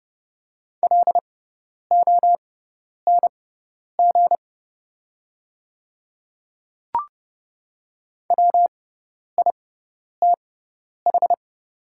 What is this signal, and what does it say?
Morse code, heard as a keyed tone, about 30 wpm, spaced as for 10 wpm, 700 Hz. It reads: LONG WITH
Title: 6x Character Spacing - Top 300 Words - 30wpm